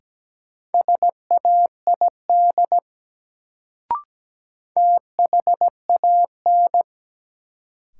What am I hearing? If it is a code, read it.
Morse code, heard as a keyed tone, about 17 wpm, 700 Hz. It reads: SAID THAN
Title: Sentences from Top 100 Words - New Word Drill - 17wpm